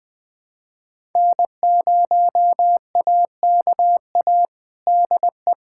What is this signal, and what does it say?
Morse code, heard as a keyed tone, about 21 wpm, 700 Hz. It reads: N0AKA DE
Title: USA Callsign DE Callsign - 5 Characters - 20wpm